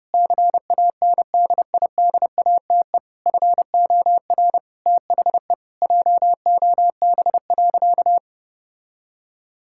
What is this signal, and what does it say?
Morse code, heard as a keyed tone, about 30 wpm, 700 Hz. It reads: CANDIDATE FOR THE JOB.
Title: Sentences from Top 1600 Words 30wpm